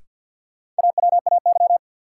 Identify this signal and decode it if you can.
Morse code, heard as a keyed tone, about 54 wpm, 700 Hz. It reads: AWAY
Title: Sets of 2 Words - New Word Drill - 50wpm